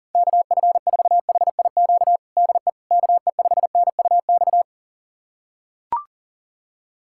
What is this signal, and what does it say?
Morse code, heard as a keyed tone, about 40 wpm, 700 Hz. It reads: KF4HIQ DE KE5NUX
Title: USA Callsign DE Callsign 40wpm